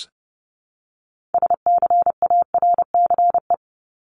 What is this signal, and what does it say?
Morse code, heard as a keyed tone, about 30 wpm, 700 Hz. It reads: SCARCE
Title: LY Suffix - Top 100 Words with Base Words 30wpm